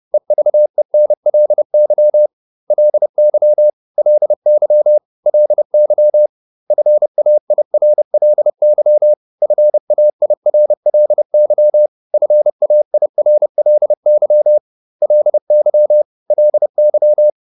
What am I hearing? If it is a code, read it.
Morse code, heard as a keyed tone, about 30 wpm, 600 Hz. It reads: EVENLY LY LY LY FAIRLY FAIRLY FAIRLY LY LY